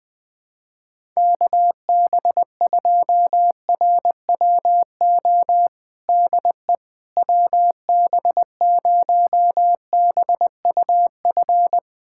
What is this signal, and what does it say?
Morse code, heard as a keyed tone, about 20 wpm, 700 Hz. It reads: KB2RWO DE WB0BUF
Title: USA Callsign DE Callsign - 6 Characters - 20wpm